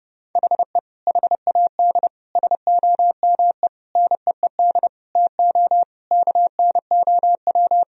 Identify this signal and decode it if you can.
Morse code, heard as a keyed tone, about 30 wpm, 700 Hz. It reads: HE HAD SOME NEED TO KNOW
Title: Sentences from Top 1000 Words - Review Part 3 - 30wpm